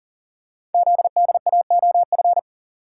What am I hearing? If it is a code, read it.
Morse code, heard as a keyed tone, about 39 wpm, 700 Hz. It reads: ZDAOF
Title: Code Groups - 5 Letter:Numbers - 40wpm